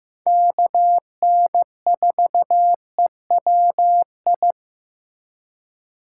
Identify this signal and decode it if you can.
Morse code, heard as a keyed tone, about 15 wpm, 700 Hz. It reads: KN4EWI
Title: K1USN SST QSO Practice - 15wpm